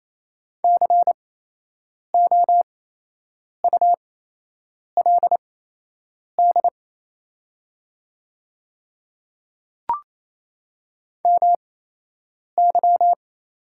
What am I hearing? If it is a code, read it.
Morse code, heard as a keyed tone, about 28 wpm, spaced as for 8 wpm, 700 Hz. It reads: COULD MY
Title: 8x Character Spacing - Top 500 Words - 28wpm